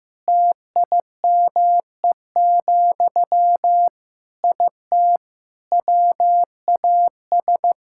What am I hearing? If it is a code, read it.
Morse code, heard as a keyed tone, about 15 wpm, 700 Hz. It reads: TIME, IT WAS
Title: Sentences from Top 1000 Words - Review Part 5 - 15wpm